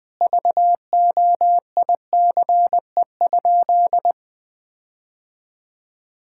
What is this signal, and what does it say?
Morse code, heard as a keyed tone, about 21 wpm, 700 Hz. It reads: VOICE?
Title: Sentences from Top 1600 Words 20wpm